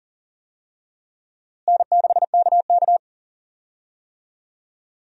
Morse code, heard as a keyed tone, about 37 wpm, 700 Hz. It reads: N6KK
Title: USA Call Signs - 4 Characters - Repeated 2x - 40wpm